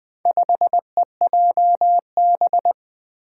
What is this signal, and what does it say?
Morse code, heard as a keyed tone, about 20 wpm, 700 Hz. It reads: 5EJB
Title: Four Challenging Letter-Number Combinations 20wpm